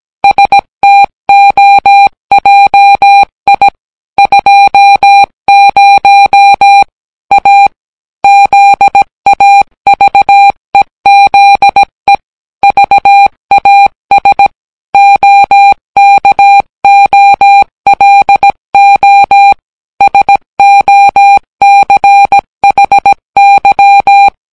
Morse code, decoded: STOJI 20 A ZAVEZE VAS OKOLO SOCHY